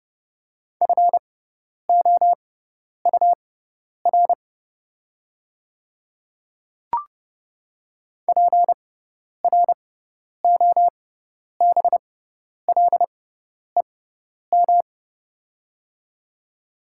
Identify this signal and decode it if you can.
Morse code, heard as a keyed tone, about 30 wpm, spaced as for 10 wpm, 690 Hz. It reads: FOUR PROBLEM
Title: 6x Character Spacing - Top 500 Words - 30wpm